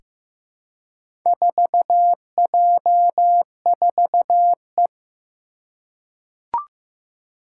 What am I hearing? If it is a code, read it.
Morse code, heard as a keyed tone, about 15 wpm, 700 Hz. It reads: 4J4E
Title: Four Challenging Letter-Number Combinations 15wpm